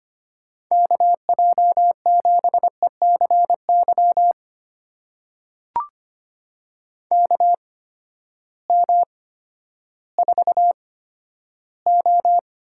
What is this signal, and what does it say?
Morse code, heard as a keyed tone, about 25 wpm, spaced as for 12 wpm, 700 Hz. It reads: KJ7ECY KM4O
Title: USA Call Signs - 6 Characters - 8x Character Spacing - 25wpm